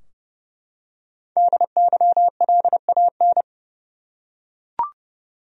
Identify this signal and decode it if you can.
Morse code, heard as a keyed tone, about 29 wpm, 700 Hz. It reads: DYLAN
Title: USA Common Names - 30wpm